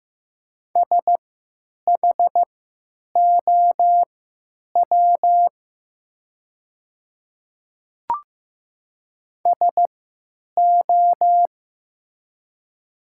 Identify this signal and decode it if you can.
Morse code, heard as a keyed tone, about 15 wpm, spaced as for 9 wpm, 700 Hz. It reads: SHOW SO
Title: 3x Character Spacing - Top 200 Words - 15wpm